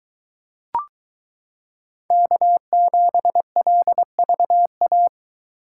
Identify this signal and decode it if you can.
Morse code, heard as a keyed tone, about 22 wpm, 700 Hz. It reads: K7LVA